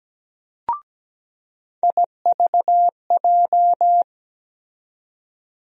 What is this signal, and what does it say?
Morse code, heard as a keyed tone, about 17 wpm, 700 Hz. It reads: IVJ
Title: USA Call Sign Postfixes - 3 Letters 17wpm